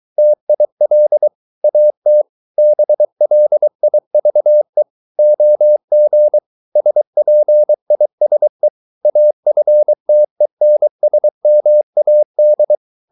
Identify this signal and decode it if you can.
Morse code, heard as a keyed tone, about 23 wpm, 600 Hz. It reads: TIL AT BLIVE OG SPISE AFTENSMAD
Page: Ferie 23 wpm